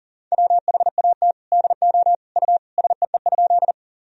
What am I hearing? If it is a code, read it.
Morse code, heard as a keyed tone, about 40 wpm, 700 Hz. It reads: WHAT DO U SEE?
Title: Sentences from Top 100 Words and Abbreviations 40wpm